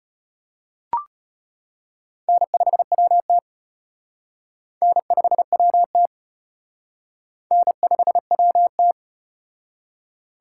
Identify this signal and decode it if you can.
Morse code, heard as a keyed tone, about 32 wpm, 700 Hz. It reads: N5WT N5WT N5WT